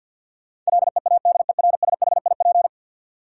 Morse code, heard as a keyed tone, about 50 wpm, 700 Hz. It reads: LEADERSHIP
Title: Sentences from Top 1600 Words - New Word Drill - 50wpm